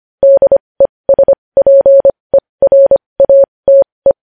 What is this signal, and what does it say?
Morse code, heard as a keyed tone, about 25 wpm, spaced as for 20 wpm, 550 Hz. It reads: DESPERATE
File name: podcast / News headlines-1x-25x20